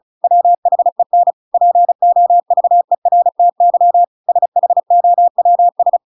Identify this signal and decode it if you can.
Morse code, heard as a keyed tone, about 34 wpm, 700 Hz. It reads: WHEN POVERTY SHOWS
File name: Pinocchio / Pinocchio ch8 35 35 ik1hge